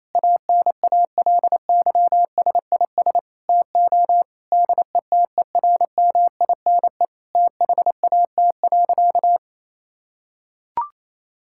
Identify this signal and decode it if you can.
Morse code, heard as a keyed tone, about 28 wpm, 700 Hz. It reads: ANALYSIS TO DETERMINE THAT.
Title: Sentences from Top 1000 Words - Review Part 5 - 28wpm